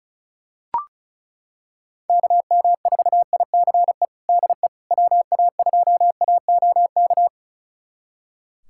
Morse code, heard as a keyed tone, about 34 wpm, 700 Hz. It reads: KM4ICE DE WA2AOK
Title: USA Callsign DE Callsign - 6 Characters - 35wpm